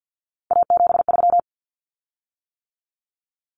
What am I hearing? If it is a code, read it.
Morse code, heard as a keyed tone, about 50 wpm, 700 Hz. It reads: A73
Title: Common Cut Numbers - 3-Digit 50wpm